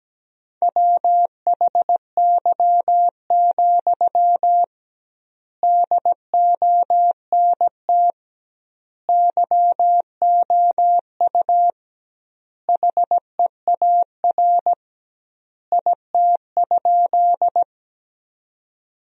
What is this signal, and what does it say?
Morse code, heard as a keyed tone, about 17 wpm, 700 Hz. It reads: WHY, DONT YOU HEAR IT?
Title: 2x Word-Spacing - Sentences from Top 300 Words - 17wpm